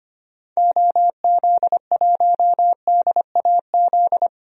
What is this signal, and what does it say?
Morse code, heard as a keyed tone, about 25 wpm, 700 Hz. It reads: OZ1DAZ
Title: DXCC-100 Call Signs - 6 Characters - 25wpm